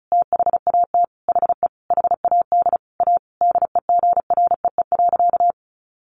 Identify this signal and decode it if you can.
Morse code, heard as a keyed tone, about 34 wpm, 700 Hz. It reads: THAT HE HAD A DEGREE.
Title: Sentences from Top 1500 Words 35wpm